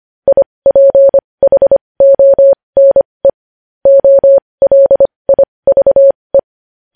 Morse code, heard as a keyed tone, about 25 wpm, spaced as for 20 wpm, 550 Hz. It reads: IPHONE OLIVE
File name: podcast / News headlines-1x-25x20